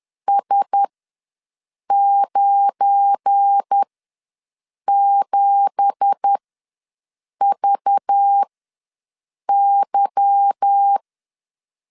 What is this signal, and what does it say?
Morse code, heard as a keyed tone, about 10 wpm, spaced as for 6 wpm, 800 Hz. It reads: S97VY
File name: LEZIONI / 9 2867 ws 10wpm 4215